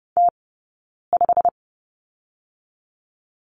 Morse code, heard as a keyed tone, about 30 wpm, 700 Hz. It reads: T 5
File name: course / 30:10wpm / 086 Lesson 15d - Number 5 - Two Characters 30wpm